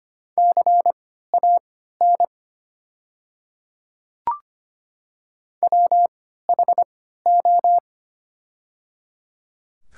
Morse code, heard as a keyed tone, about 25 wpm, spaced as for 14 wpm, 700 Hz. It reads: CAN WHO
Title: 3x Character Spacing - Top 500 Words - 25wpm